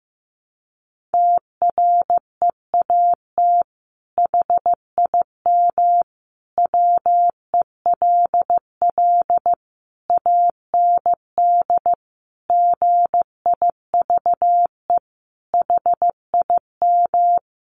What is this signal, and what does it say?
Morse code, heard as a keyed tone, about 15 wpm, 700 Hz. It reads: TREAT HIM WELL AND GIVE HIM
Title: Sentences from Top 1300 Words 15wpm